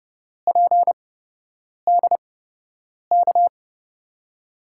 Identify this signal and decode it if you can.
Morse code, heard as a keyed tone, about 30 wpm, spaced as for 8 wpm, 700 Hz. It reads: PDK
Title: USA Call Signs - 5 Characters - 8x Character Spacing - 30wpm